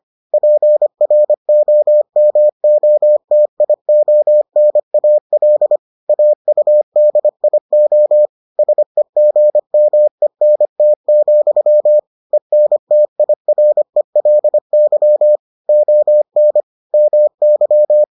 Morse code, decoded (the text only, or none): PROMOTIONAL AUDIO SEGMENT, ENTIRELY ON MY